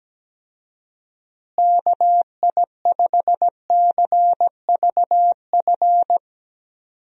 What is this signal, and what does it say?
Morse code, heard as a keyed tone, about 17 wpm, 700 Hz. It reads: KI5CVF
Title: USA Call Signs - 6 Characters - Repeated 2x - 17wpm